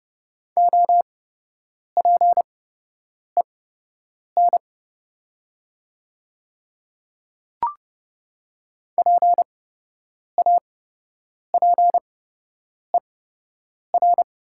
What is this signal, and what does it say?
Morse code, heard as a keyed tone, about 30 wpm, spaced as for 8 wpm, 700 Hz. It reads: OPEN PAPER
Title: 8x Character Spacing - Top 500 Words - 30wpm